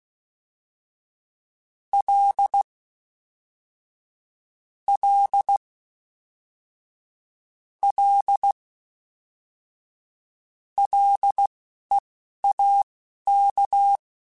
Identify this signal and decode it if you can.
Morse code, heard as a keyed tone, about 16 wpm, spaced as for 11 wpm, 800 Hz. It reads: L L L LEAK